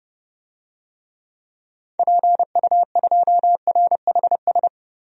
Morse code, heard as a keyed tone, about 30 wpm, 700 Hz. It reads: PU2RHS
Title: DXCC-100 Call Signs - 6 Characters - 30wpm